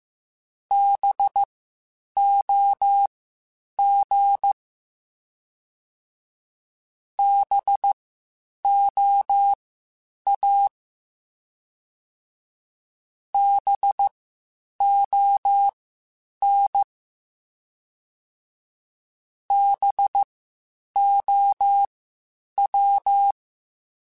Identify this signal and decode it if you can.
Morse code, heard as a keyed tone, about 15 wpm, spaced as for 8 wpm, 800 Hz. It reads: BOG BOA BON BOW
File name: Spaced Alphas / Spaced Alphas mp3s / Bs sp @13